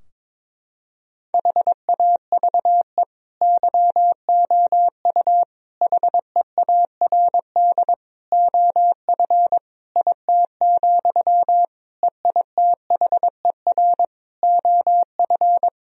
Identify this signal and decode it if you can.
Morse code, heard as a keyed tone, about 22 wpm, 700 Hz. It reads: HAVE YOU HEARD OF IT, EITHER OF
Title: Sentences from Top 1000 Words - Review Part 4 - 22wpm